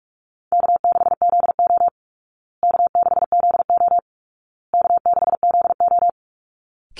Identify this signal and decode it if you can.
Morse code, heard as a keyed tone, about 44 wpm, 700 Hz. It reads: K6ZO K6ZO K6ZO
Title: Easy Call Signs Repeated 3x - 45wpm